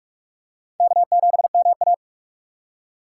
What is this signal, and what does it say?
Morse code, heard as a keyed tone, about 40 wpm, 700 Hz. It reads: K7MA